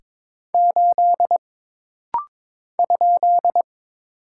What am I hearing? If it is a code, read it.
Morse code, heard as a keyed tone, about 22 wpm, 700 Hz. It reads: 8 ?
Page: This rapid-fire timing is geared to push your ICR (Instant Character Recognition) skills.